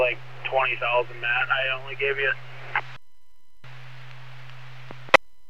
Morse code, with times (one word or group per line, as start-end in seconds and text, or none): none